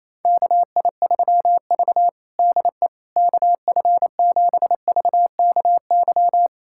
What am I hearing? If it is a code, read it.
Morse code, heard as a keyed tone, about 28 wpm, 700 Hz. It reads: KI3V DE KF7VKY
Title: USA Callsign DE Callsign 28wpm